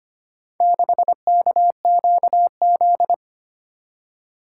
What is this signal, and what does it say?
Morse code, heard as a keyed tone, about 25 wpm, 700 Hz. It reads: 6KQZ
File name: practice / 25wpm / Four Letter-Number Combinations - 25wpm